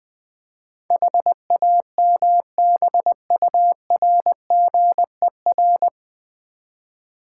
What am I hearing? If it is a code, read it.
Morse code, heard as a keyed tone, about 20 wpm, 700 Hz. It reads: HAMBURGER
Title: Easy Words with Hints - 20wpm